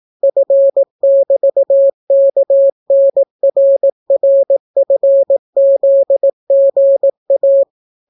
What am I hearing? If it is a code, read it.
Morse code, heard as a keyed tone, about 18 wpm, 550 Hz. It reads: F<BT>KNRRFZGA